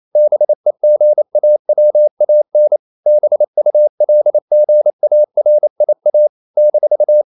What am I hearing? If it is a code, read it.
Morse code, heard as a keyed tone, about 28 wpm, 600 Hz. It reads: BEGAWAN BULGARIA -
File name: CW / Practice / NationalCapitals28